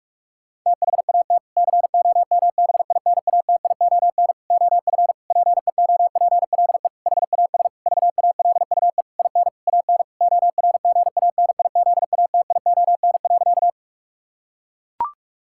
Morse code, heard as a keyed tone, about 41 wpm, 700 Hz. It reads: THAT COMBINATION OF PEOPLE HAS VALUE IN AN ORGANIZATION.
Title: Sentences from Top 1000 Words - Review Part 6 - 45wpm